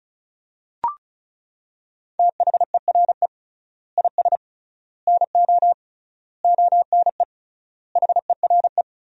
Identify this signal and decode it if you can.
Morse code, heard as a keyed tone, about 34 wpm, 700 Hz. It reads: THERE IS NO ONE HERE
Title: 3x Word-Spacing - Sentences from Top 200 Words - 35wpm